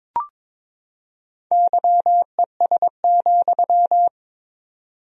Morse code, heard as a keyed tone, about 22 wpm, 700 Hz. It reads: YES,
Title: 4x Word-Spacing - Sentences from Top 500 Words - 22wpm